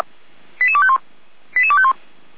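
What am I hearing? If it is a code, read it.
Five-tone selective call (ZVEI), heard as quick runs of five tones, tones of about 70 ms. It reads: 88251 88251
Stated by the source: Five-tone ZVEI selective call 88102, received by SDR.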